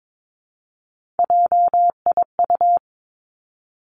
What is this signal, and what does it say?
Morse code, heard as a keyed tone, about 22 wpm, 700 Hz. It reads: JIU
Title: Three Challenging Letter-Number Combinations 22wpm